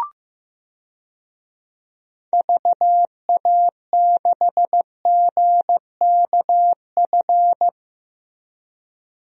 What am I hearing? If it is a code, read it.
Morse code, heard as a keyed tone, about 15 wpm, 700 Hz. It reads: VA6GKF
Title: DXCC-100 Call Signs - 6 Characters - Repeated 3x - 15wpm